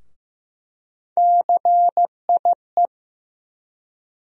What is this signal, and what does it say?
Morse code, heard as a keyed tone, about 15 wpm, 700 Hz. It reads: CIE
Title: USA Call Sign Postfixes - 3 Letters 15wpm